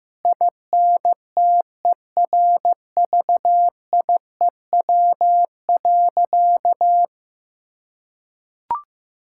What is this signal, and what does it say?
Morse code, heard as a keyed tone, about 15 wpm, 700 Hz. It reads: INTERVIEW.
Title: Sentences from Top 1600 Words 15wpm